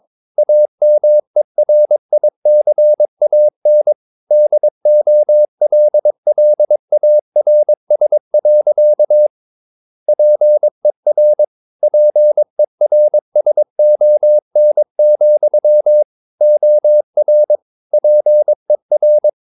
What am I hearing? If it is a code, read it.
Morse code, heard as a keyed tone, about 22 wpm, 600 Hz. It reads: AMERICAN DOLLARS. PER PERSON, OR PER
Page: Romanen Never af Ken Follet på engelsk.